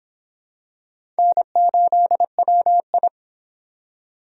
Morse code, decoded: N8WI

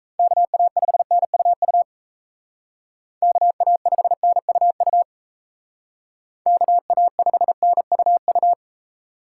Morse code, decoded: KA5NUU KA5NUU KA5NUU

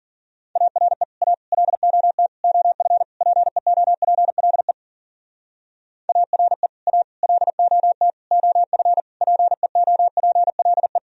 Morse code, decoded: ARE A LOT OF PEOPLE ARE A LOT OF PEOPLE

